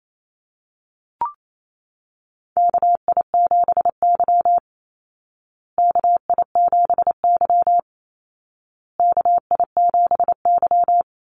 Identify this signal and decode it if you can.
Morse code, heard as a keyed tone, about 28 wpm, 700 Hz. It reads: KI7Y KI7Y KI7Y
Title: Call Signs Repeated 2x - 28wpm